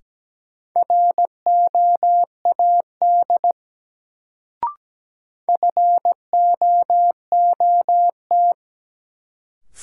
Morse code, heard as a keyed tone, about 17 wpm, 700 Hz. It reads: ROAD FOOT
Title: Sets of 5 Words - New Word Drill - 17wpm